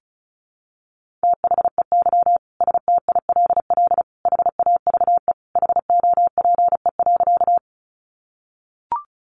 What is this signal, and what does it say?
Morse code, heard as a keyed tone, about 34 wpm, 700 Hz. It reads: THEY STILL HAVE HOPE.
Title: Sentences from Top 1000 Words 35wpm